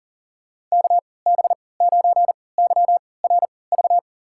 Morse code, heard as a keyed tone, about 40 wpm, spaced as for 23 wpm, 700 Hz. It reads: KB9YRV
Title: USA Call Signs - 3x Character Spacing - 40wpm